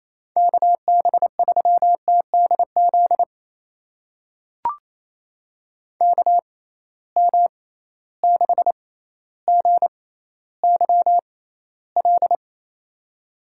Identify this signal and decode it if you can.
Morse code, heard as a keyed tone, about 27 wpm, spaced as for 14 wpm, 700 Hz. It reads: KB3TDZ KM6GYL
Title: USA Call Signs - 6 Characters - 6x Character Spacing - 28wpm